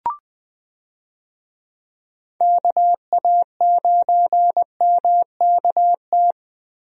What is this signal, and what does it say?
Morse code, heard as a keyed tone, about 20 wpm, 700 Hz. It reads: KA9MKT
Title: USA Call Signs - 6 Characters - Repeated 2x - 20wpm